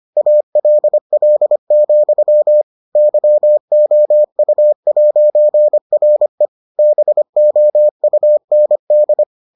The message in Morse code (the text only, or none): ALL, YOU'RE BOUND